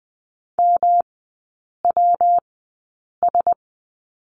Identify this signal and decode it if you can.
Morse code, heard as a keyed tone, about 20 wpm, spaced as for 8 wpm, 700 Hz. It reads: MWS
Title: 152 Lesson 25e - Letter p - Three Characters 20wpm